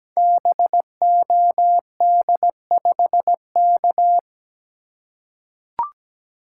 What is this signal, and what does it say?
Morse code, heard as a keyed tone, about 17 wpm, 700 Hz. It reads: BOD5K
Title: Code Groups - 5 Letter:Numbers - 17wpm